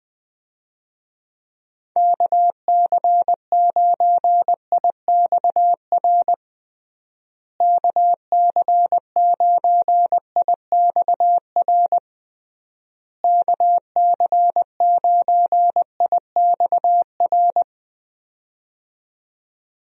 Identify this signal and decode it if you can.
Morse code, heard as a keyed tone, about 21 wpm, 700 Hz. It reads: KC9IXR KC9IXR KC9IXR
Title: USA Call Signs - 6 Characters - Repeated 3x - 20wpm